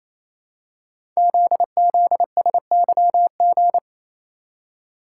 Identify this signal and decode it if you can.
Morse code, heard as a keyed tone, about 27 wpm, 700 Hz. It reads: ZZSYG
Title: Code Groups - 5 Letters - 28wpm